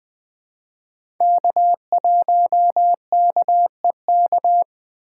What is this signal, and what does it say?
Morse code, heard as a keyed tone, about 20 wpm, 700 Hz. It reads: K1KEK
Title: USA Call Signs - 6x Character Spacing - 20wpm